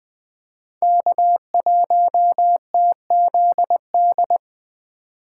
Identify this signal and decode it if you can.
Morse code, heard as a keyed tone, about 20 wpm, 700 Hz. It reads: K1TZD
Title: Speed Racing - USA Call Signs 17wpm